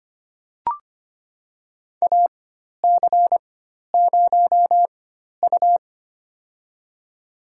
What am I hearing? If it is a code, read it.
Morse code, heard as a keyed tone, about 25 wpm, spaced as for 12 wpm, 700 Hz. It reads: AC0U
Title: USA Call Signs - 4 Characters - 4x Character Spacing - 25wpm